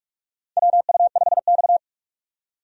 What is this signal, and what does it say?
Morse code, heard as a keyed tone, about 46 wpm, 700 Hz. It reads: WU5X
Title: USA Call Signs - 4 Characters - 45wpm